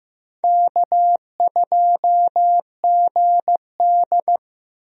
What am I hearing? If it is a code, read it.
Morse code, heard as a keyed tone, about 15 wpm, 700 Hz. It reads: K2GD
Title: USA Call Signs - 4 Characters - Repeated 2x - 15wpm